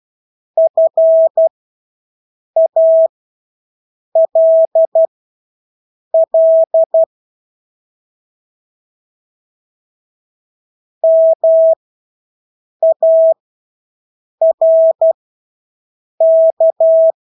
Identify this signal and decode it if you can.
Morse code, heard as a keyed tone, about 12 wpm, spaced as for 6 wpm, 650 Hz. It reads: FALL MARK
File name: Engelske ord 12-6 wpm